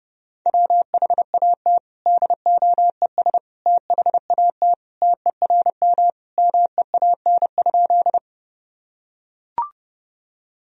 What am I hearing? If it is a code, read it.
Morse code, heard as a keyed tone, about 30 wpm, 700 Hz. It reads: WHAT DOES THAT TERM MEAN?
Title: Sentences from Top 1000 Words - Review Part 4 - 30wpm